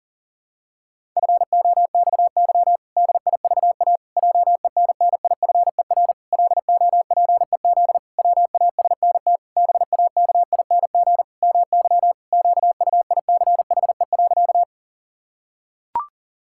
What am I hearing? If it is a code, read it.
Morse code, heard as a keyed tone, about 40 wpm, 700 Hz. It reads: FOXY DIVA JENNIFER LOPEZ WASNT BAKING MY QUICHE.
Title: Pangram Challenge v2 - 40wpm